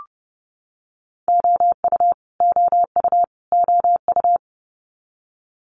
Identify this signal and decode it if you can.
Morse code, heard as a keyed tone, about 30 wpm, 700 Hz. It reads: OU OU OU
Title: Multi-letter Phonemes Repeated 3x 30wpm